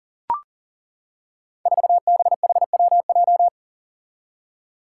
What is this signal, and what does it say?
Morse code, heard as a keyed tone, about 40 wpm, 700 Hz. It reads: 4BHWJ
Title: Five Challenging Letter-Number Combinations 40wpm